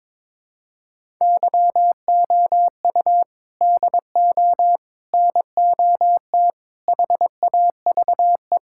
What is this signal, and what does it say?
Morse code, heard as a keyed tone, about 22 wpm, 700 Hz. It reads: YOU DO NOT HAVE
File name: practice / 22wpm / Sentences with 4-Word Phrases - 22wpm